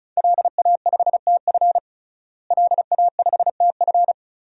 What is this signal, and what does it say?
Morse code, heard as a keyed tone, about 36 wpm, 700 Hz. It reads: LA5TF LA5TF
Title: DXCC-100 Call Signs - 5 Characters - Repeated 2x - 35wpm